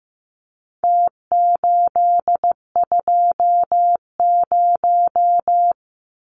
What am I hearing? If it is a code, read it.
Morse code, heard as a keyed tone, about 15 wpm, 700 Hz. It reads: T820
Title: Numbers - Contest Style - 4-Digit 15wpm